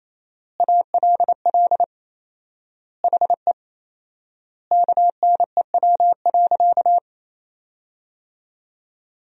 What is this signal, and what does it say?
Morse code, heard as a keyed tone, about 27 wpm, 700 Hz. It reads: ALL HE KNEW.
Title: 4x Word-Spacing - Sentences from Top 400 Words - 28wpm